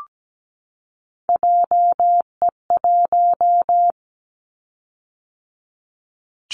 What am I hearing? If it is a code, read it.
Morse code, heard as a keyed tone, about 17 wpm, 700 Hz. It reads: JE1
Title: Callsign Prefixes for Top 25 DXCC - 17wpm